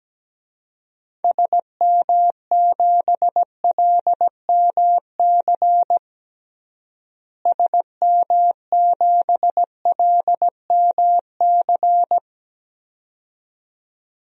Morse code, heard as a keyed tone, about 17 wpm, 700 Hz. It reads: SM7LMC SM7LMC
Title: DXCC-100 Call Signs - 6 Characters - Repeated 2x - 17wpm